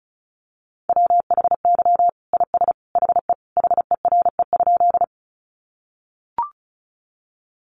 Morse code, heard as a keyed tone, about 34 wpm, 700 Hz. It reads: WHY IS HE HERE?
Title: Sentences from Top 200 Words 35wpm